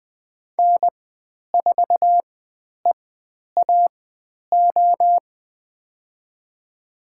Morse code, heard as a keyed tone, about 20 wpm, spaced as for 10 wpm, 700 Hz. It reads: N4EAO
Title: 095 Lesson 16g - Letter c - Call Signs 20wpm